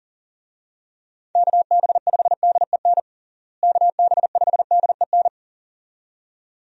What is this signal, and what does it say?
Morse code, heard as a keyed tone, about 40 wpm, 700 Hz. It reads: KB5DEN KB5DEN
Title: USA Call Signs - 6 Characters - Repeated 2x - 40wpm